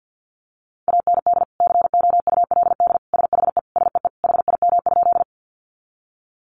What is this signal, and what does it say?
Morse code, heard as a keyed tone, about 50 wpm, 700 Hz. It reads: AND COULD SHE SEE HIM?
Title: Sentences from Top 100 Words 50wpm